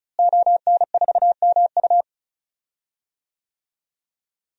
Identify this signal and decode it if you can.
Morse code, heard as a keyed tone, about 36 wpm, 700 Hz. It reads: ON4MU
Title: DXCC-100 Call Signs - 5 Characters - Repeated 3x - 35wpm